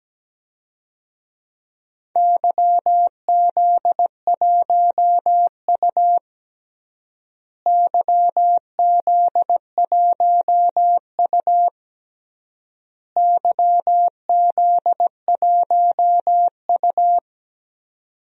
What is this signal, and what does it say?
Morse code, heard as a keyed tone, about 17 wpm, 700 Hz. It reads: YZ1U YZ1U YZ1U
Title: DXCC-100 Call Signs - 4 Characters - Repeated 3x - 17wpm